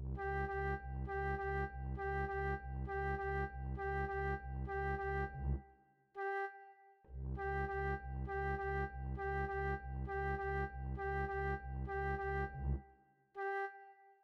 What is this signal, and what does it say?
There is no code.